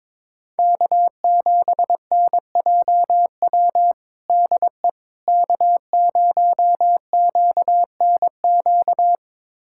Morse code, heard as a keyed tone, about 22 wpm, 700 Hz. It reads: K7NJW DE K0QNQ
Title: USA Callsign DE Callsign - 5 Characters - 22wpm